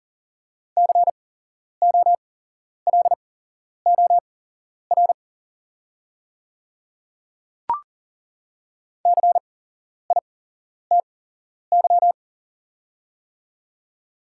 Morse code, heard as a keyed tone, about 40 wpm, spaced as for 11 wpm, 700 Hz. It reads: COLOR CITY
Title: 8x Character Spacing - Top 300 Words - 40wpm